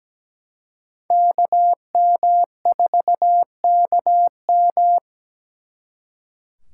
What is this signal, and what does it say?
Morse code, heard as a keyed tone, about 20 wpm, 700 Hz. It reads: KM4KM
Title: Speed Racing - USA Call Signs 17wpm